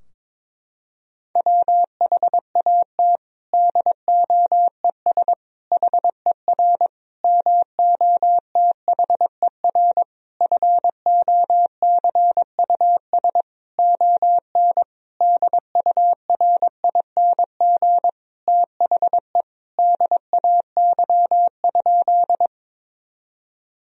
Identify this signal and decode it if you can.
Morse code, heard as a keyed tone, about 22 wpm, 700 Hz. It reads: WHAT DOES HER MOTHER FOCUS ON DURING THE DAY?
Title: Sentences from Top 1000 Words - Review Part 4 - 22wpm